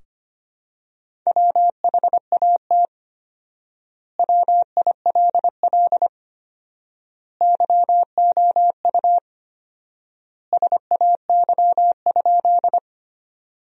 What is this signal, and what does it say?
Morse code, heard as a keyed tone, about 25 wpm, 700 Hz. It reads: WHAT WILL YOU SAY?
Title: 4x Word-Spacing - Sentences from Top 200 Words - 25wpm